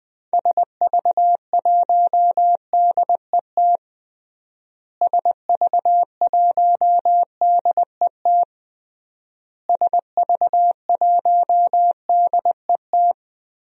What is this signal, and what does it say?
Morse code, heard as a keyed tone, about 21 wpm, 700 Hz. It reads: SV1DET SV1DET SV1DET
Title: DXCC-100 Call Signs - 6 Characters - Repeated 3x - 20wpm